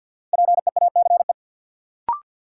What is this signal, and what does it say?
Morse code, heard as a keyed tone, about 51 wpm, 700 Hz. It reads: PEACE
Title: Sentences from Top 1500 Words - New Word Drill - 50wpm